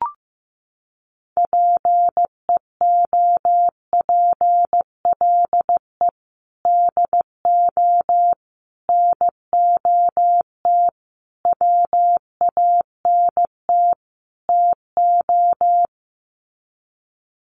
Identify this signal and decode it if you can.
Morse code, heard as a keyed tone, about 15 wpm, 700 Hz. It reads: PEOPLE DO NOT WANT TO
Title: Sets of 5 Words Encore2 15wpm